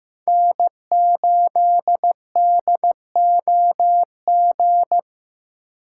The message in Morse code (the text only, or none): N8DOG